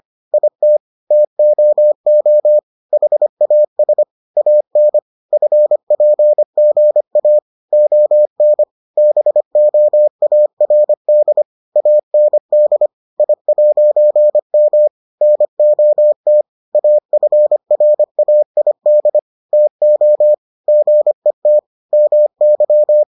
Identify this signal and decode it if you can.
Morse code, heard as a keyed tone, about 25 wpm, 600 Hz. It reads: IT TOO HAS AN FPGA ON BOARD AND I'M NOT AFRAID TO GET MY